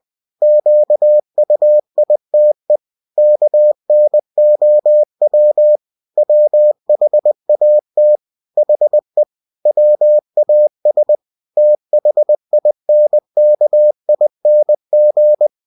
Morse code, heard as a keyed tone, about 20 wpm, 600 Hz. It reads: QUITE KNOW WHAT HE WAS THINKING
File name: cw-books / winnie-the-pooh / winnie-20wpm-page-0026